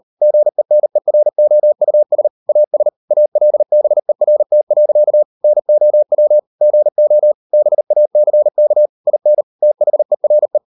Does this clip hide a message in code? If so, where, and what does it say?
Yes: Morse code from 0.2 to 10.6 s, heard as a keyed tone, about 38 wpm, 600 Hz. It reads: GENEROUS AS ALBERT. NOW GO BACK IN THERE